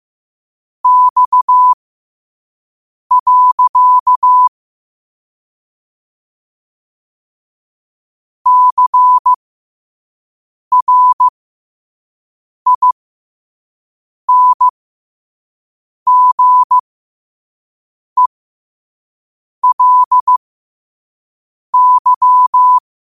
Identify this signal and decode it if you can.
Morse code, heard as a keyed tone, about 15 wpm, spaced as for 5 wpm, 1000 Hz. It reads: X. CRINGELY
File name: qotdfiles / 5wpm